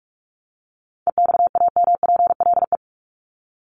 Morse code, heard as a keyed tone, about 44 wpm, 700 Hz. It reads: EXAMPLE
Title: Top 500 Words 45wpm